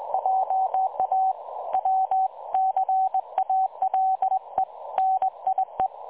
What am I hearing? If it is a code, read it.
Morse code, heard as a keyed tone, about 20 wpm, 760 Hz. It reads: JA WCALE NIE
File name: dlazaawansowanych / nagraniazpasma / websdr recording 3565.0kHz